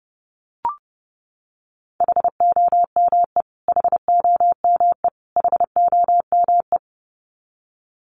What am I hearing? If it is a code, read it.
Morse code, heard as a keyed tone, about 30 wpm, 700 Hz. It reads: HOME HOME HOME
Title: Top 500 Words Repeated 30wpm